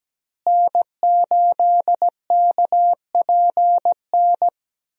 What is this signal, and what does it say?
Morse code, heard as a keyed tone, about 17 wpm, 700 Hz. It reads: N8KPN
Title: USA Call Signs - 5 Characters - Repeated 3x - 17wpm